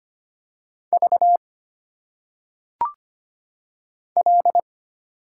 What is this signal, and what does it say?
Morse code, heard as a keyed tone, about 25 wpm, 700 Hz. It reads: V L
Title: Single Letters - Phonetic Pronunciation 25wpm